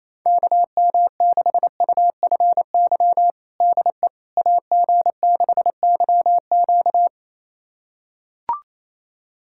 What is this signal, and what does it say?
Morse code, heard as a keyed tone, about 28 wpm, 700 Hz. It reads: KM6UFY DE AG6YQ
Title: USA Callsign DE Callsign 28wpm